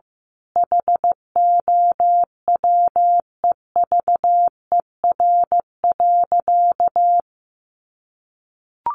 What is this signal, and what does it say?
Morse code, heard as a keyed tone, about 15 wpm, 700 Hz. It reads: HOWEVER.
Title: Sentences from Top 1100 Words 15wpm